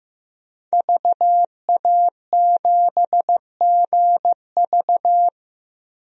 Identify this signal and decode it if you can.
Morse code, heard as a keyed tone, about 15 wpm, 700 Hz. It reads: VA7GV
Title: DXCC-100 Call Signs - 5 Characters - Repeated 2x - 15wpm